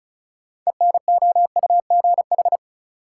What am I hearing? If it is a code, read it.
Morse code, heard as a keyed tone, about 36 wpm, 700 Hz. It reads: ENOUGH
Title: Six Letter Words - 35wpm